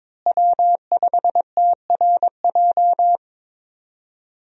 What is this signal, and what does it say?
Morse code, heard as a keyed tone, about 22 wpm, 700 Hz. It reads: W5TRJ
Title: USA Call Signs - 5 Characters - Repeated 3x - 22wpm